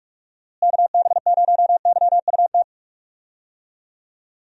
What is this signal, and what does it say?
Morse code, heard as a keyed tone, about 46 wpm, 700 Hz. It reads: KB0YUT
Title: USA Call Signs - 6 Characters - Repeated 2x - 45wpm